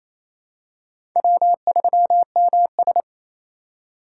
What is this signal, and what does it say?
Morse code, heard as a keyed tone, about 28 wpm, 700 Hz. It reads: W3MS
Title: K1USN SST QSO Practice - 28wpm